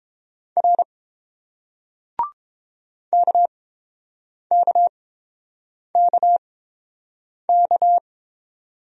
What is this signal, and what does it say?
Morse code, heard as a keyed tone, about 31 wpm, spaced as for 7 wpm, 700 Hz. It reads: R KKKK